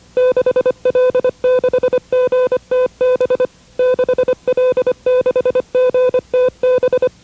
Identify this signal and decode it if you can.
Morse code, heard as a keyed tone, about 25 wpm, 1500 Hz, with noise. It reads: 6L6GTB 6L6GTB
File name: WRTC1990-CW-Pileup